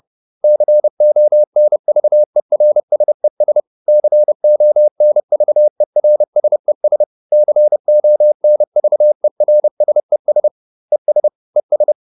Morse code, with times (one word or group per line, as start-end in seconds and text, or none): 0.4-3.6 CONVERSES
3.9-7.0 CONVERSES
7.3-10.5 CONVERSES
10.9-11.3 ES
11.6-11.9 ES